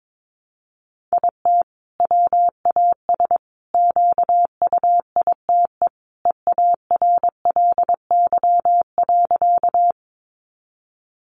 Morse code, heard as a keyed tone, about 22 wpm, 700 Hz. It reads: IT WAS QUITE EARLY.
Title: Sentences from Top 700 Words 22wpm